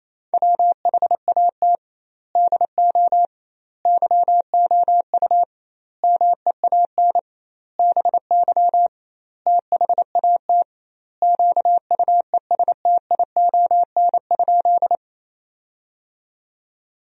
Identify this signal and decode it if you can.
Morse code, heard as a keyed tone, about 28 wpm, 700 Hz. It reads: WHAT DO YOU MEAN BY THAT QUESTION?
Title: 2x Word-Spacing - Sentences from Top 400 Words - 28wpm